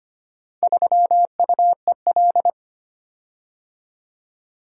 Morse code, heard as a keyed tone, about 25 wpm, 700 Hz. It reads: 3UEL
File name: practice / 25wpm / Four Letter-Number Combinations - 25wpm